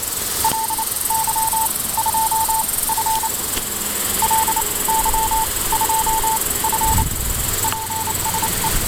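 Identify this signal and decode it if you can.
Morse code, heard as a keyed tone, about 27 wpm, spaced as for 20 wpm, 890 Hz, with noise. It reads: LY2F LY2F PSE
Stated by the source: Morse QSO on 7030 kHz between LY2F and UA1CEG using a homemade superheterodyne receiver with converter, operated outdoors; strong wind noise audible at times.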